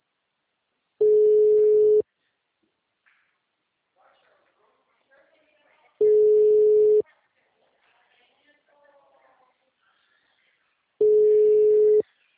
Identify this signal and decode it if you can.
Ringback tone (Europe, 425 Hz), with speech.
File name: out- -2007- - - .270